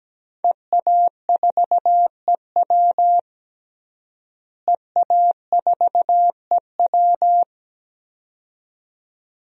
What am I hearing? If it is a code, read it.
Morse code, heard as a keyed tone, about 17 wpm, 700 Hz. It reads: EA4EW EA4EW
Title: DXCC-100 Call Signs - 5 Characters - Repeated 2x - 17wpm